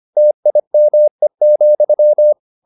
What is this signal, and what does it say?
Morse code, heard as a keyed tone, about 25 wpm, 600 Hz. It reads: TIME,